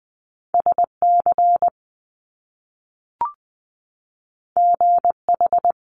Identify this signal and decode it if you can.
Morse code, heard as a keyed tone, about 20 wpm, 700 Hz. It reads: SC GH
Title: Top Two Letter Combinations in Words 20wpm